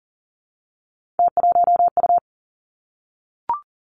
Morse code, heard as a keyed tone, about 38 wpm, 700 Hz. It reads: T1U
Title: Three Challenging Letter-Number Combinations 40wpm